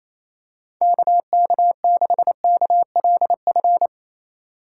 Morse code, heard as a keyed tone, about 28 wpm, 700 Hz. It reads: KK6KLF
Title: USA Call Signs - 6 Characters - 3x Character Spacing - 28wpm